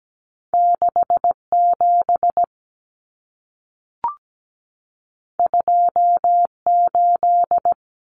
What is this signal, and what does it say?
Morse code, heard as a keyed tone, about 17 wpm, 700 Hz. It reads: 67 28
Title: Common Cut Numbers - 2-Digit 17wpm